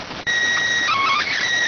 SSTV (slow-scan television), PD 290.